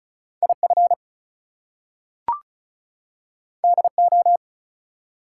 Morse code, heard as a keyed tone, about 36 wpm, 700 Hz. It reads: IF DO
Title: Two Letter Words - 35wpm